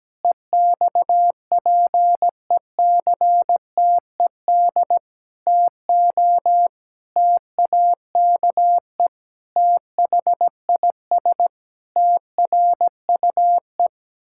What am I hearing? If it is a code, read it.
Morse code, heard as a keyed tone, about 17 wpm, 700 Hz. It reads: EXPECTED TO TAKE THIS TRUE
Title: Pangram Challenge v2 - 17wpm